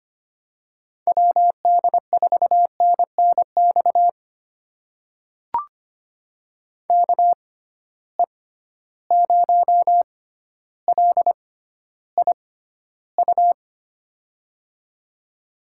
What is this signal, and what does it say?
Morse code, heard as a keyed tone, about 25 wpm, spaced as for 13 wpm, 700 Hz. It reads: WD4NNX KE0LIU
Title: USA Call Signs - 6 Characters - 6x Character Spacing - 25wpm